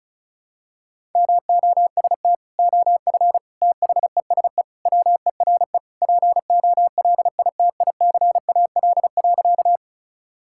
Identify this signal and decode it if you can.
Morse code, heard as a keyed tone, about 36 wpm, 700 Hz. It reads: MOST OF THESE WERE POLITICAL.
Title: Sentences from Top 1000 Words - Review Part 5 - 35wpm